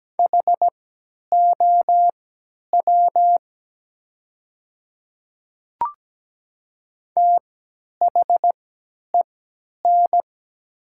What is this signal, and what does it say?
Morse code, heard as a keyed tone, about 17 wpm, spaced as for 10 wpm, 700 Hz. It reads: HOW THEN
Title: 3x Character Spacing - Top 100 Words - 17wpm